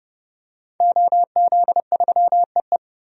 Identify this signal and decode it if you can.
Morse code, heard as a keyed tone, about 32 wpm, 700 Hz. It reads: OZ3EE
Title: DXCC-100 Call Signs - 5 Characters - 30wpm